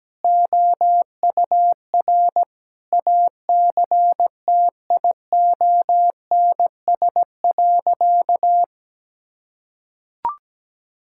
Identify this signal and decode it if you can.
Morse code, heard as a keyed tone, about 17 wpm, 700 Hz. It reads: OUR ACTIONS.
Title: Sentences from Top 1000 Words - Review Part 6 - 17wpm